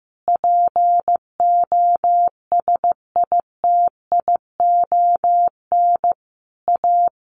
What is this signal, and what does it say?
Morse code, heard as a keyed tone, about 15 wpm, 700 Hz. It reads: POSITION A
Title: Sentences from Top 1200 Words 15wpm